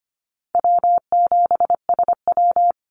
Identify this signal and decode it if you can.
Morse code, heard as a keyed tone, about 25 wpm, 700 Hz. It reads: W7SW
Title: Easy Call Signs Repeated 2x - 25wpm